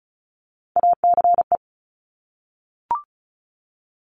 Morse code, heard as a keyed tone, about 36 wpm, 700 Hz. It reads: ACE
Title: Top Three Letter Combinations in Words 35wpm